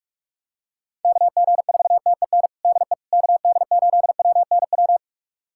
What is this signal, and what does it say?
Morse code, heard as a keyed tone, about 41 wpm, 700 Hz. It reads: KG4TEN DE KD8WNW
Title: USA Callsign DE Callsign - 6 Characters - 45wpm